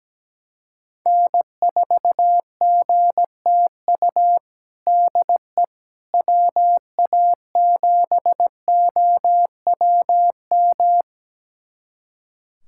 Morse code, heard as a keyed tone, about 17 wpm, 700 Hz. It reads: N4GTU DE WA7OWM
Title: USA Callsign DE Callsign 17wpm